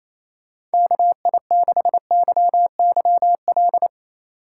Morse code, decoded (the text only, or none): KI6YYL